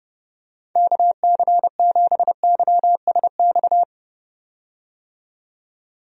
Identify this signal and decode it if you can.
Morse code, heard as a keyed tone, about 32 wpm, 700 Hz. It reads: KC7YSX
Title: USA Call Signs - 6 Characters - 30wpm